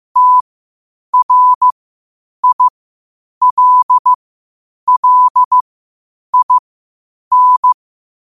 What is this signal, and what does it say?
Morse code, heard as a keyed tone, about 15 wpm, spaced as for 8 wpm, 1000 Hz. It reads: TRILLIN
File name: qotdfiles / 8wpm